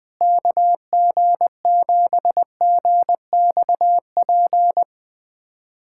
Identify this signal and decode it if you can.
Morse code, heard as a keyed tone, about 20 wpm, 700 Hz. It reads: KG7GXP
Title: USA Callsign DE Callsign - 6 Characters - 20wpm